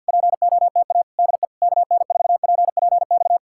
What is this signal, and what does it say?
Morse code, heard as a keyed tone, about 64 wpm, spaced as for 56 wpm, 700 Hz. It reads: POTA DE KN4PPX
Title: Speed-Racing Common QSO Phrases Encore - 50wpm